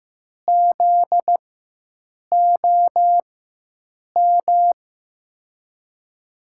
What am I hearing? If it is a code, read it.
Morse code, heard as a keyed tone, about 15 wpm, spaced as for 7 wpm, 700 Hz. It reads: ZOM
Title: USA Call Signs - 6 Characters - 4x Character Spacing - 15wpm